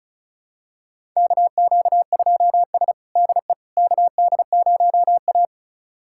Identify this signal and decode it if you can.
Morse code, heard as a keyed tone, about 36 wpm, 700 Hz. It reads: KQ2S DE KD0A
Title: USA Callsign DE Callsign - 4 Characters - 35wpm